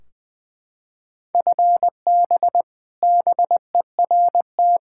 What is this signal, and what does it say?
Morse code, heard as a keyed tone, about 20 wpm, 700 Hz. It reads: FB BERT
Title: Practice QSO - Exchange 3 - More Info - 20wpm